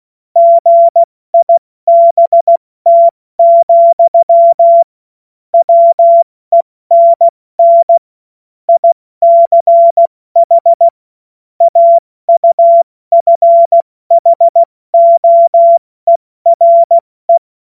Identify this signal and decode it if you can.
Morse code, heard as a keyed tone, about 16 wpm, 680 Hz. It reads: GIBT, WENN ICH AUFHOERE
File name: Morsekurs / Witze / Witz 16. Mai 2023 70 F80